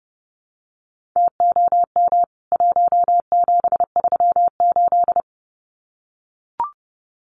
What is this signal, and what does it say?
Morse code, heard as a keyed tone, about 30 wpm, 700 Hz. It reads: TOM 1738
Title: CWT Exchange 30wpm